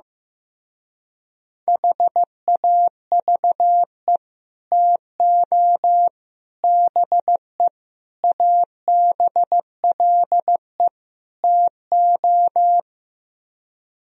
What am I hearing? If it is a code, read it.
Morse code, heard as a keyed tone, about 15 wpm, 700 Hz. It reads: HAVE TO BE ABLE TO
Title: Speed-Racing - Sets of 5 Words 15wpm